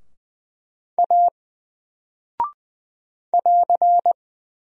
Morse code, heard as a keyed tone, about 20 wpm, 700 Hz. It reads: A <AR>
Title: Single Character (Letter:Number:Punctuation:Prosign) 20wpm